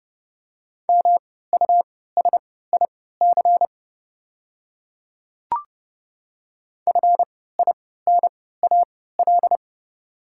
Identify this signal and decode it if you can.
Morse code, heard as a keyed tone, about 30 wpm, spaced as for 17 wpm, 700 Hz. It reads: MUSIC FINAL
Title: 3x Character Spacing - Top 500 Words - 30wpm